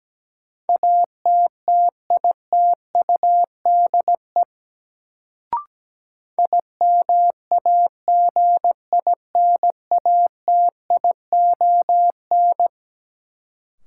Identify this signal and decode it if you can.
Morse code, heard as a keyed tone, about 17 wpm, 700 Hz. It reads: ATTITUDE IMAGINATION
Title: Sentences from Top 1300 Words - New Word Drill - 17wpm